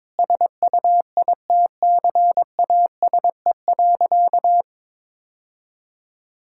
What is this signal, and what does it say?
Morse code, heard as a keyed tone, about 22 wpm, 700 Hz. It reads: SUITCASE.
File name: practice / 22wpm / Easy Sentences with Hints - 22wpm